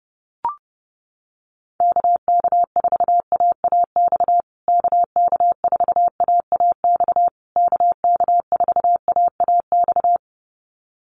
Morse code, heard as a keyed tone, about 30 wpm, 700 Hz. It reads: KK4AAX KK4AAX KK4AAX
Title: USA Call Signs Repeated 3x 30wpm